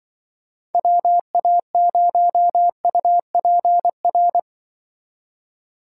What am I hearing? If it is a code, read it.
Morse code, heard as a keyed tone, about 25 wpm, 700 Hz. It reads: WA0UPR
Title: Speed Racing - USA Call Signs 20wpm